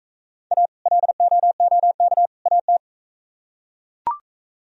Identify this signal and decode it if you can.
Morse code, heard as a keyed tone, about 40 wpm, 700 Hz. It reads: A LOOK AT